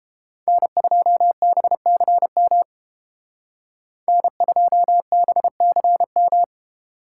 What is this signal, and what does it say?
Morse code, heard as a keyed tone, about 30 wpm, 700 Hz. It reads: N2BCM N2BCM